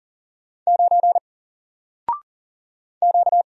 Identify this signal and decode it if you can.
Morse code, heard as a keyed tone, about 40 wpm, 700 Hz. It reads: 9 Q
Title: Single Letter-Number Alt 40wpm